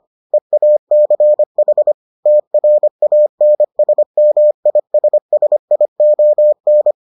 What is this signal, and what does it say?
Morse code, heard as a keyed tone, about 25 wpm, 600 Hz. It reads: EACH TRANSMISSION